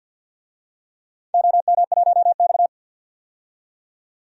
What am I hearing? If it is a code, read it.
Morse code, heard as a keyed tone, about 51 wpm, 700 Hz. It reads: OM1X
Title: DXCC-100 Call Signs - 4 Characters - 50wpm